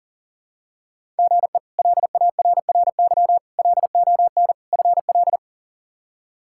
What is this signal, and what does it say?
Morse code, heard as a keyed tone, about 40 wpm, 700 Hz. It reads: GE LARRY LON FL
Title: K1USN SST QSO Practice - 40wpm